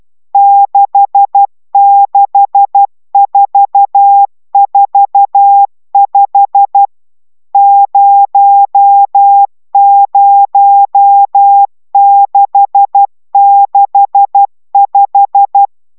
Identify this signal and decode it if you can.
Morse code, heard as a keyed tone, about 12 wpm, 800 Hz. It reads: 66445 00665
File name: taviro tanulas / cw 69 20 10